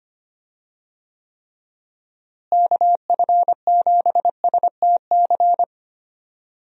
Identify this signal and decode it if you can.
Morse code, heard as a keyed tone, about 25 wpm, 700 Hz. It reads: KF7STC
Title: USA Call Signs - 6 Characters - Repeated 3x - 25wpm